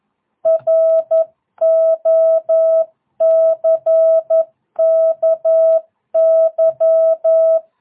Morse code, heard as a keyed tone, about 11 wpm, 660 Hz. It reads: ROCKY
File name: jh8cbh / 2022cw